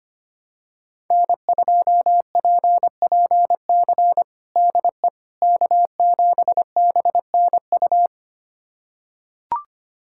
Morse code, heard as a keyed tone, about 25 wpm, 700 Hz. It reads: N2PPC DE K7BNU
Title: USA Callsign DE Callsign - 5 Characters - 25wpm